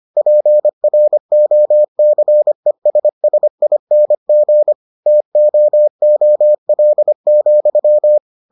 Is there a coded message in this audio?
Yes: Morse code from 0.2 to 8.2 s, heard as a keyed tone, about 25 wpm, 600 Hz. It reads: PROCESSING TOOL,